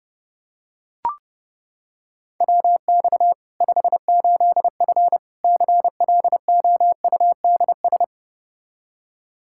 Morse code, heard as a keyed tone, about 32 wpm, 700 Hz. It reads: WX 58F CLOUDS
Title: Practice QSO - Exchange 2 - Info - 30wpm